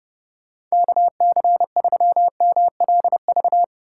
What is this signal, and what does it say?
Morse code, heard as a keyed tone, about 31 wpm, 700 Hz. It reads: KC3MLV
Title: USA Call Signs - 6 Characters - 2x Character Spacing - 30wpm